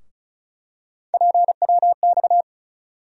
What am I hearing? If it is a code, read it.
Morse code, heard as a keyed tone, about 34 wpm, 700 Hz. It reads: PWX
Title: USA Call Sign Postfixes - 3 Letters 35wpm